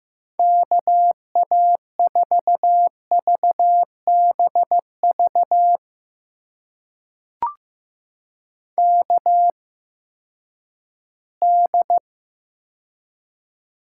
Morse code, heard as a keyed tone, about 15 wpm, spaced as for 10 wpm, 700 Hz. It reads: KA4VBV KD
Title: USA Call Signs - 6 Characters - 8x Character Spacing - 15wpm